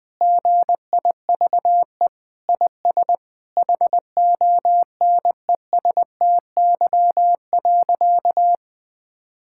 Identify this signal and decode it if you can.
Morse code, heard as a keyed tone, about 20 wpm, 700 Hz. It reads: GIVE IS HONESTY.
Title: Sentences with 8-Word Phrases - 20wpm